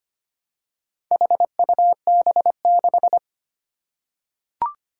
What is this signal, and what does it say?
Morse code, heard as a keyed tone, about 25 wpm, 700 Hz. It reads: HUB6
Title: Four Challenging Letter-Number Combinations 25wpm